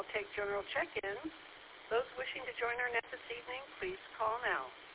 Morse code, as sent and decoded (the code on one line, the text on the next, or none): none